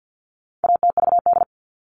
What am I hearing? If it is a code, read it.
Morse code, heard as a keyed tone, about 48 wpm, 700 Hz. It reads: ATVD
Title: All Cut Numbers - 4-Digit 50wpm